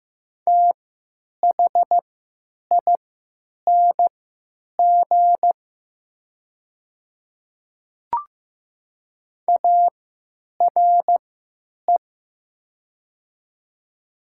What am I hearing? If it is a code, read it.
Morse code, heard as a keyed tone, about 15 wpm, spaced as for 9 wpm, 700 Hz. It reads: THING ARE
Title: 3x Character Spacing - Top 200 Words - 15wpm